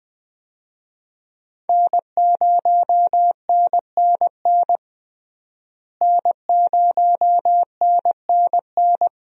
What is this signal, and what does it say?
Morse code, heard as a keyed tone, about 21 wpm, 700 Hz. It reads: N0NNN N0NNN
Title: USA Call Signs - 5 Characters - Repeated 3x - 20wpm